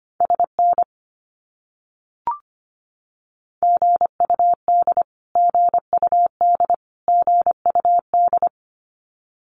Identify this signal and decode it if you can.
Morse code, heard as a keyed tone, about 25 wpm, 700 Hz. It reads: SN GUD GUD GUD
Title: Abbreviations Repeated 3x - 25wpm